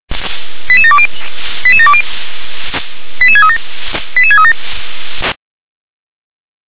Five-tone selective call (ZVEI), heard as quick runs of five tones, tones of about 70 ms. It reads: 88720 88720 88637 88637